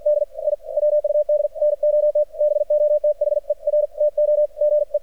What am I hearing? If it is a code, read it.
Morse code, heard as a keyed tone, about 53 wpm, 600 Hz, with radio noise. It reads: D I WUD NOT BOTHER TO GE
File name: archived / CRSnet